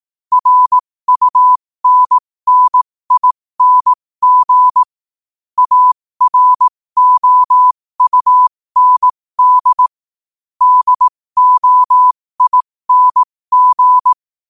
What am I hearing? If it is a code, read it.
Morse code, heard as a keyed tone, about 18 wpm, spaced as for 15 wpm, 1000 Hz. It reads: RUNNING AROUND DOING
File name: qotdfiles / 15wpm